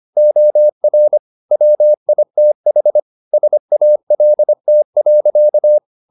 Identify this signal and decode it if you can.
Morse code, heard as a keyed tone, about 25 wpm, 600 Hz. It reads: OR WITH SALT.
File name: the old man and the sea part 1 25wpm